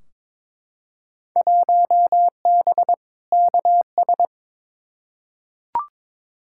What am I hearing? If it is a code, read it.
Morse code, heard as a keyed tone, about 22 wpm, 700 Hz. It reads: 1B KS
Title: Field Day Exchanges Encore - 22wpm